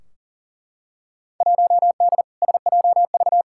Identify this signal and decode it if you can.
Morse code, heard as a keyed tone, about 40 wpm, 700 Hz. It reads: 1D SJV
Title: Field Day Exchanges Encore - 40wpm